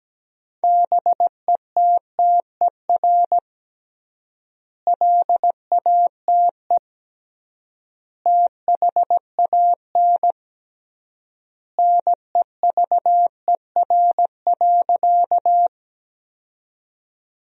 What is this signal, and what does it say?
Morse code, heard as a keyed tone, about 17 wpm, 700 Hz. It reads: BETTER LATE THAN NEVER.
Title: 3x Word-Spacing - Sentences from Top 400 Words - 17wpm